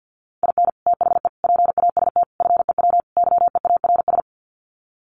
Morse code, heard as a keyed tone, about 50 wpm, 700 Hz. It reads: IN THE PAST FEW YEARS
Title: Sets of 5 Words 50wpm